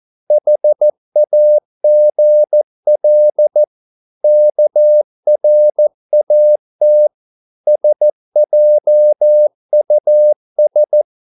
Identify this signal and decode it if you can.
Morse code, heard as a keyed tone, about 14 wpm, 600 Hz. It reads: HAGL KRAT SJUS
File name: Korte ord DK 14wpm